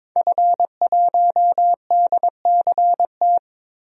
Morse code, heard as a keyed tone, about 22 wpm, 700 Hz. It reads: F1DCT
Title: DXCC-100 Call Signs - 5 Characters - 22wpm